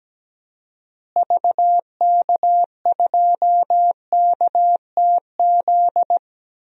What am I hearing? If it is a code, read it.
Morse code, heard as a keyed tone, about 17 wpm, 700 Hz. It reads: VK2KTZ
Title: DXCC-100 Call Signs - 6 Characters - Repeated 2x - 17wpm